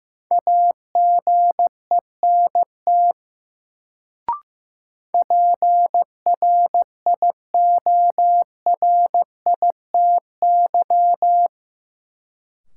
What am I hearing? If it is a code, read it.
Morse code, heard as a keyed tone, about 15 wpm, 700 Hz. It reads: AGENT PRIORITY
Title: Sentences from Top 1600 Words - New Word Drill - 15wpm